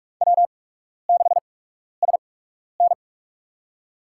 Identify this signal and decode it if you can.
Morse code, heard as a keyed tone, about 46 wpm, spaced as for 12 wpm, 700 Hz. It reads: W6SN
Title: USA Call Signs - 8x Character Spacing - 45wpm